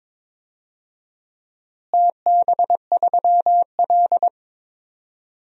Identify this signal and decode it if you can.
Morse code, heard as a keyed tone, about 22 wpm, 700 Hz. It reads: TB3L
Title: DXCC-100 Call Signs - 4 Characters - Repeated 3x - 22wpm